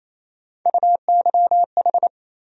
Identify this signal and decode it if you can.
Morse code, heard as a keyed tone, about 28 wpm, 700 Hz. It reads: UYH
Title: USA Call Sign Postfixes - 3 Letters 28wpm